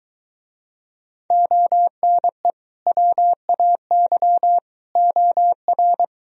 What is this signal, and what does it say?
Morse code, heard as a keyed tone, about 22 wpm, 700 Hz. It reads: ONE WAY OR